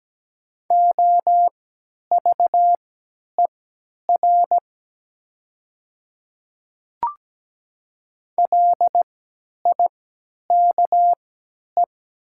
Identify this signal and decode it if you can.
Morse code, heard as a keyed tone, about 17 wpm, spaced as for 10 wpm, 700 Hz. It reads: OVER LIKE
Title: 3x Character Spacing - Top 100 Words - 17wpm